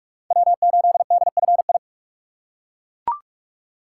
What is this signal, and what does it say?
Morse code, heard as a keyed tone, about 41 wpm, 700 Hz. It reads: W8DFI
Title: USA Call Signs - 3x Character Spacing - 45wpm